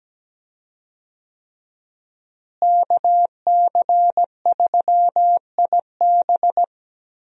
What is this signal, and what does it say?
Morse code, heard as a keyed tone, about 17 wpm, 700 Hz. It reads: KC3IB
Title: USA Call Signs - 5 Characters - Repeated 3x - 17wpm